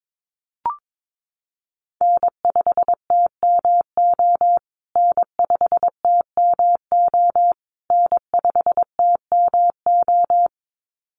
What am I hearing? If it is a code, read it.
Morse code, heard as a keyed tone, about 22 wpm, 700 Hz. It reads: N5TMO N5TMO N5TMO
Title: USA Call Signs Repeated 3x 22wpm